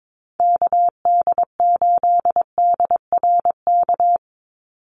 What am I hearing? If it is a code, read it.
Morse code, heard as a keyed tone, about 22 wpm, 700 Hz. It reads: KD8DRK
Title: USA Call Signs 22wpm